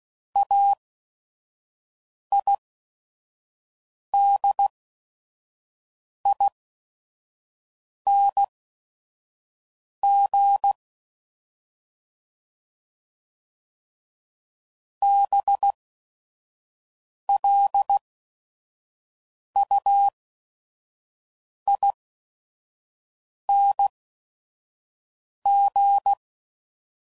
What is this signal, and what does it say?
Morse code, heard as a keyed tone, about 16 wpm, spaced as for 5 wpm, 800 Hz. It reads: AIDING BLUING
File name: Suffixes / Special ING's / VerySlowToFaster ING's